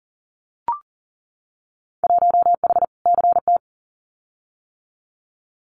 Morse code, heard as a keyed tone, about 40 wpm, 700 Hz. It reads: 1H CT
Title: Winter Field Day Exchanges Encore 40wpm